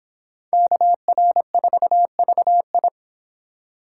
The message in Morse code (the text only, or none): KR4VI